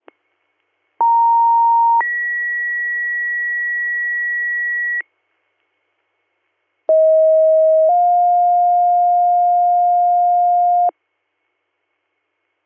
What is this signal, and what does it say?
Two-tone page, heard as a one-second tone, then a three-second tone, about 930 then 1950 Hz, about 640 then 710 Hz. It is heard from 1.0 to 10.9 s.